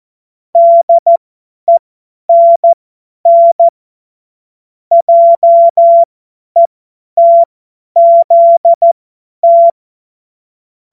Morse code, heard as a keyed tone, about 14 wpm, spaced as for 10 wpm, 680 Hz. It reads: DENN JETZT
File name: Morsekurs / Witze / Witz 07.November 2023 50 F70